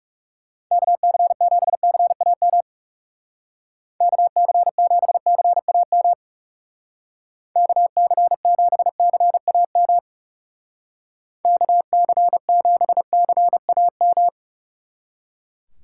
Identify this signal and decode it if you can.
Morse code, heard as a keyed tone, about 38 wpm, 700 Hz. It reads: KC7CAM KC7CAM KC7CAM KC7CAM